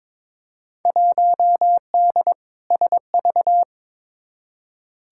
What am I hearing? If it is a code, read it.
Morse code, heard as a keyed tone, about 22 wpm, 700 Hz. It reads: 1D SV
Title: Field Day Exchanges - 22wpm